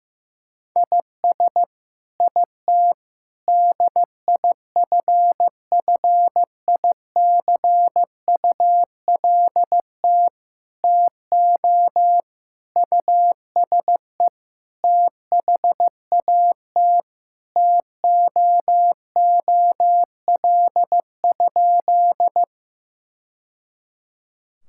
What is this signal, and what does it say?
Morse code, heard as a keyed tone, about 15 wpm, 700 Hz. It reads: IS IT DIFFICULT TO USE THAT TOOL?
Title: Sentences from Top 1000 Words - Review Part 6 - 15wpm